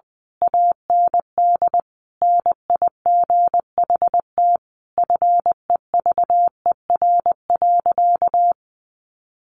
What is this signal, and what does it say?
Morse code, heard as a keyed tone, about 20 wpm, 700 Hz. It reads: AND NIGHT FEVER.
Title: A Sound Of Thunder 20wpm